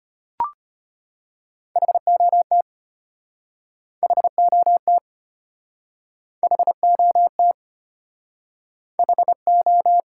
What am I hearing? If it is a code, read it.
Morse code, heard as a keyed tone, about 33 wpm, 700 Hz. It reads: HOT HOT HOT HO